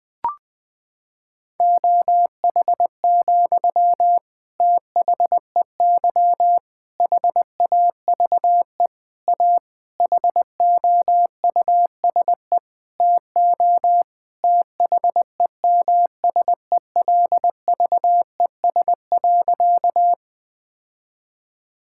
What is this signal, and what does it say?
Morse code, heard as a keyed tone, about 20 wpm, 700 Hz. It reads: OH, THEY HAVE A HOUSE TO THEMSELVES.
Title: Sentences from Top 1000 Words - Review Part 5 - 20wpm